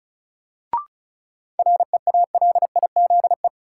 Morse code, heard as a keyed tone, about 36 wpm, 700 Hz. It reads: REALIZE
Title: Sentences from Top 1000 Words - New Word Drill - 35wpm